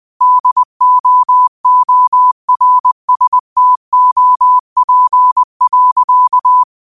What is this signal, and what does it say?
Morse code, heard as a keyed tone, about 20 wpm, 1000 Hz. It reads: DOORSTOP.
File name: 20wpm